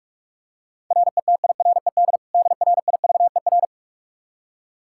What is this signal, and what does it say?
Morse code, heard as a keyed tone, about 42 wpm, 700 Hz. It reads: RETIRED DRIVER
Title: Practice QSO - Exchange 3 - More Info - 45wpm